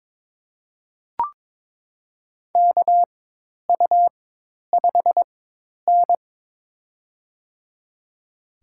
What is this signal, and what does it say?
Morse code, heard as a keyed tone, about 22 wpm, spaced as for 10 wpm, 700 Hz. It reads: KU5N
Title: USA Call Signs - 4 Characters - 4x Character Spacing - 22wpm